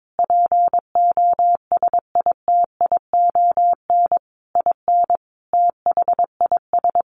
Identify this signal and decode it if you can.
Morse code, heard as a keyed tone, about 22 wpm, 700 Hz. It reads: POSITION IN THIS
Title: Sentences from Top 1200 Words 22wpm